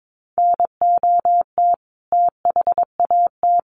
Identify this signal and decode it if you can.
Morse code, heard as a keyed tone, about 22 wpm, 700 Hz. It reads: NOT THAT
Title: A Sound Of Thunder 22wpm